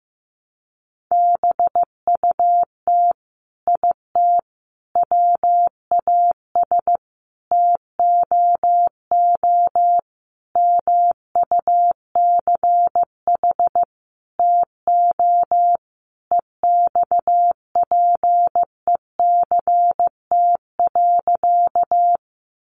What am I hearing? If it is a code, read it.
Morse code, heard as a keyed tone, about 15 wpm, 700 Hz. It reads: BUT IT WAS TOO MUCH TO EXPECT.
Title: Sentences from Top 1100 Words 15wpm